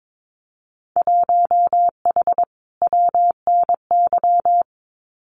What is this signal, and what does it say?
Morse code, heard as a keyed tone, about 22 wpm, 700 Hz. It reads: 1H WNY
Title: Winter Field Day Exchanges 22wpm